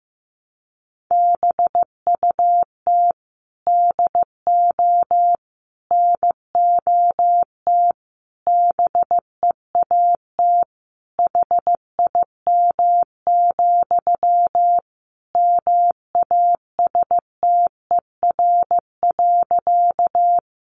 Morse code, heard as a keyed tone, about 15 wpm, 700 Hz. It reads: BUT DO NOT BEAT HIM, MASTER.
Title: Sentences from Top 1300 Words 15wpm